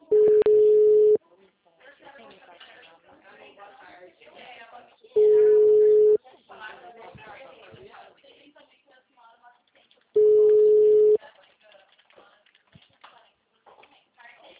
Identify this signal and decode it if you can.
Ringback tone (Europe, 425 Hz), with speech.
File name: out- -2032- - - .69889